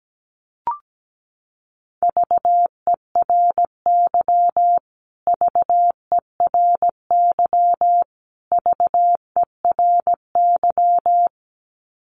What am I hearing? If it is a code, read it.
Morse code, heard as a keyed tone, about 17 wpm, 700 Hz. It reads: VERY VERY VERY
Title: Top 500 Words Repeated 17wpm